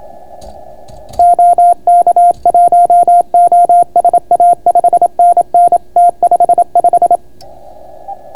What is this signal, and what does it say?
Morse code, heard as a keyed tone, about 28 wpm, 700 Hz. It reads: OK1OSA 5NN T55